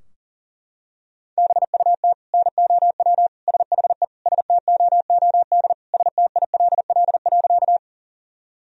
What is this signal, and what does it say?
Morse code, heard as a keyed tone, about 40 wpm, 700 Hz. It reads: BUT NOW SHE STOOD STILL.
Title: Sentences from Top 1000 Words - Review Part 2 - 40wpm